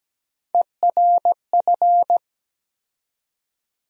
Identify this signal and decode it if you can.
Morse code, heard as a keyed tone, about 17 wpm, 700 Hz. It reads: ERF
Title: USA Call Sign Postfixes - 3 Letters 17wpm